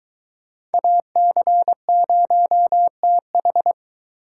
Morse code, heard as a keyed tone, about 23 wpm, 700 Hz. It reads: AC0TH